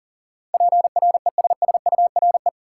Morse code, heard as a keyed tone, about 38 wpm, 700 Hz. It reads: PRESSURE
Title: Sentences from Top 1000 Words - Review Part 6 - New Word Drill - 40wpm